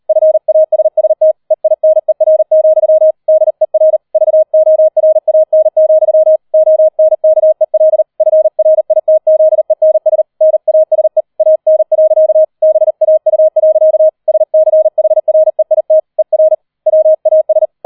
Morse code, heard as a keyed tone, about 38 wpm, 630 Hz. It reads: FASST EINER, DER VORAN, ONKEL FRITZENS NASE AN. BAU. SCHREIT ER WAS
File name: 40wpm